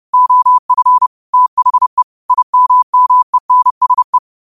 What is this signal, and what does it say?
Morse code, heard as a keyed tone, about 30 wpm, 1000 Hz. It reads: OF THE IMMENSE
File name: qotdfiles / 30wpm